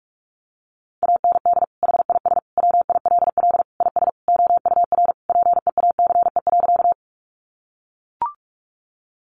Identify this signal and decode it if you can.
Morse code, heard as a keyed tone, about 44 wpm, 700 Hz. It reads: AND HIS WILL IS OUR PEACE.
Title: Sentences from Top 1500 Words 45wpm